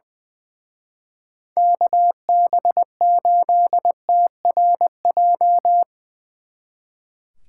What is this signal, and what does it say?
Morse code, heard as a keyed tone, about 22 wpm, 700 Hz. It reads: KB8TRJ
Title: Speed Racing - USA 2x3 Call Signs 20wpm